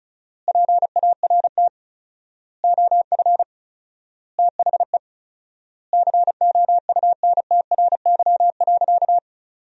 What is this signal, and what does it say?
Morse code, heard as a keyed tone, about 36 wpm, 700 Hz. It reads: PART OF THE COUNTRY.
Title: 4x Word-Spacing - Sentences from Top 300 Words - 35wpm